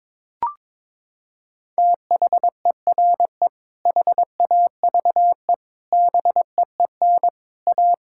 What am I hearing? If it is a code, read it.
Morse code, heard as a keyed tone, about 22 wpm, 700 Hz. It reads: THERE HAVE BEEN A
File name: practice / 22wpm / Sets of 6 Words - 22wpm